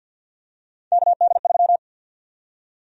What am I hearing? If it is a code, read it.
Morse code, heard as a keyed tone, about 53 wpm, 700 Hz. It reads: KD3
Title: USA Call Sign Prefixes - Repeated 2x - 50wpm